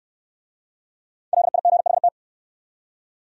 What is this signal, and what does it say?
Morse code, heard as a keyed tone, about 67 wpm, 700 Hz. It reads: LIGHT